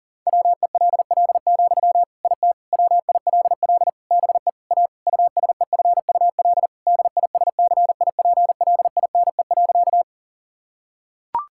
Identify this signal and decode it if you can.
Morse code, heard as a keyed tone, about 37 wpm, 700 Hz. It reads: WELL, IT WILL BE A USEFUL DISCIPLINE.
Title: Sentences from Top 1000 Words - Review Part 6 - 40wpm